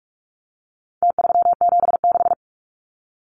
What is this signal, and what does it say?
Morse code, heard as a keyed tone, about 44 wpm, 700 Hz. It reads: T376
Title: Numbers - Contest Style - 4-Digit 45wpm